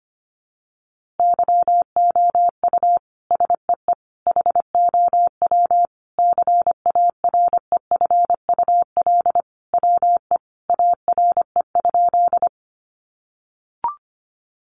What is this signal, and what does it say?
Morse code, heard as a keyed tone, about 25 wpm, 700 Hz. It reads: YOU SEE HOW CAREFUL WE ARE?
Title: A Sound Of Thunder 25wpm